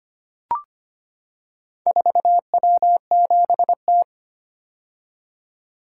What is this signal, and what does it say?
Morse code, heard as a keyed tone, about 25 wpm, 700 Hz. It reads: 4W7T
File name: practice / 25wpm / Four Letter-Number Combinations - 25wpm